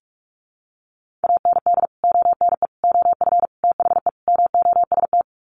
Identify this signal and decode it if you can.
Morse code, heard as a keyed tone, about 44 wpm, 700 Hz. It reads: AND ONE OF THE MOST
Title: Sets of 5 Words Encore 45wpm